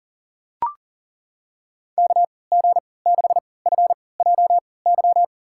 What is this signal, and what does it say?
Morse code, heard as a keyed tone, about 40 wpm, spaced as for 23 wpm, 700 Hz. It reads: KG6FJY
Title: USA Call Signs - 6 Characters - 3x Character Spacing - 40wpm